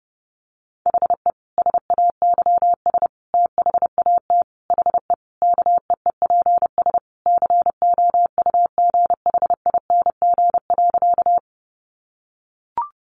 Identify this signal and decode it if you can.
Morse code, heard as a keyed tone, about 30 wpm, 700 Hz. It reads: HE SAYS THAT HE KEEPS COUGHING.
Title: Additional Easy Phrases 30wpm